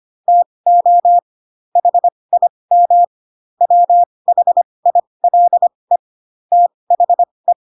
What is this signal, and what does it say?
Morse code, heard as a keyed tone, about 25 wpm, spaced as for 20 wpm, 700 Hz. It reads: TO HIM WHILE THE
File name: Pinocchio / Pinocchio ch7 15 25 ik1hge